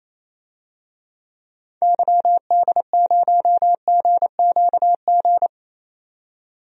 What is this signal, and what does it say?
Morse code, heard as a keyed tone, about 28 wpm, 700 Hz. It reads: YD0GQG
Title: DXCC-100 Call Signs - 6 Characters - 28wpm